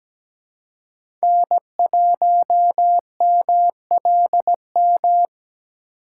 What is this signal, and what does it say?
Morse code, heard as a keyed tone, about 17 wpm, 700 Hz. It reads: N1MLM
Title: USA Call Signs - 5 Characters - 4x Character Spacing - 17wpm